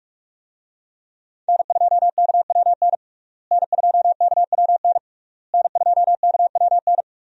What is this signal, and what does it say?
Morse code, heard as a keyed tone, about 47 wpm, 700 Hz. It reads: N2KWN N2KWN N2KWN
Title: USA Call Signs - 5 Characters - Repeated 3x - 45wpm